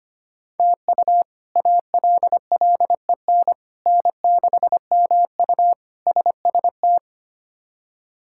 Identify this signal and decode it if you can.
Morse code, heard as a keyed tone, about 25 wpm, 700 Hz. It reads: TU ALLEN N6MU SST
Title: K1USN SST QSO Practice - 25wpm